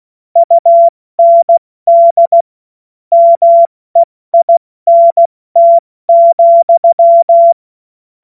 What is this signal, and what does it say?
Morse code, heard as a keyed tone, about 16 wpm, 680 Hz. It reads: UND MEINT,
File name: Morsekurs / Witze / Witz 16. Dezember 2025 70 F80